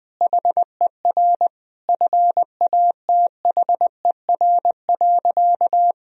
Morse code, heard as a keyed tone, about 20 wpm, 700 Hz. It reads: HER FATHER.
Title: Sentences from Top 1000 Words - Review Part 3 - 20wpm